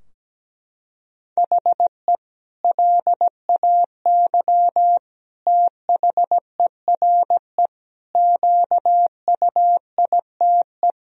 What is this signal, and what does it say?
Morse code, heard as a keyed tone, about 17 wpm, 700 Hz. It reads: HE LAY THERE QUITE
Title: Sentences from Top 1000 Words - Review Part 2 - 17wpm